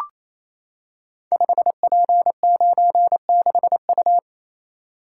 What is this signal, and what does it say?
Morse code, heard as a keyed tone, about 29 wpm, 700 Hz. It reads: 5P96U
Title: Code Groups - 5 Letter:Numbers - 28wpm